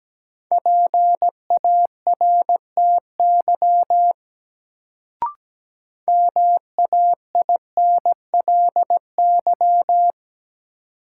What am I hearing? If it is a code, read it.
Morse code, heard as a keyed tone, about 17 wpm, 700 Hz. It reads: PARTY MAINLY
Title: Sentences from Top 1400 Words - New Word Drill - 17wpm